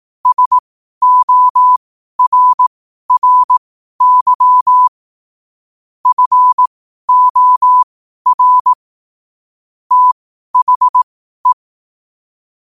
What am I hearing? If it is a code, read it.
Morse code, heard as a keyed tone, about 18 wpm, spaced as for 12 wpm, 1000 Hz. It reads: SORRY FOR THE
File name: qotdfiles / 12wpm